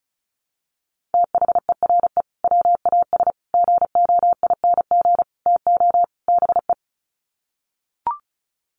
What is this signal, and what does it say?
Morse code, heard as a keyed tone, about 34 wpm, 700 Hz. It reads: THERE WAS GOING TO BE
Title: Sets of 5 Words Encore2 35wpm